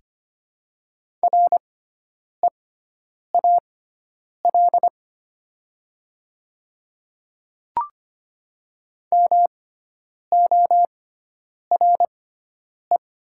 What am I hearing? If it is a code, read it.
Morse code, heard as a keyed tone, about 25 wpm, spaced as for 9 wpm, 700 Hz. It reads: REAL MORE
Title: 6x Character Spacing - Top 300 Words - 25wpm